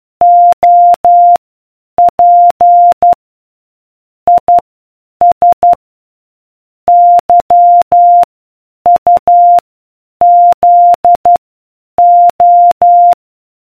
Morse code, heard as a keyed tone, about 12 wpm, spaced as for 8 wpm, 700 Hz. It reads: OP IS YUZO